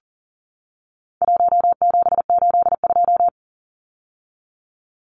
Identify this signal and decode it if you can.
Morse code, heard as a keyed tone, about 40 wpm, 700 Hz. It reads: 1782
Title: Numbers - 4-Digit 40wpm